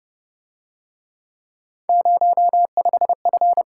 Morse code, heard as a keyed tone, about 32 wpm, 700 Hz. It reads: 05F